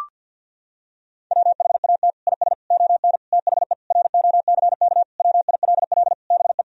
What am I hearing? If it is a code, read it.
Morse code, heard as a keyed tone, about 50 wpm, 700 Hz. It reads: WHAT IS ON THE ROCK WILL BE
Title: Sentences from Top 1000 Words - Review Part 2 - 50wpm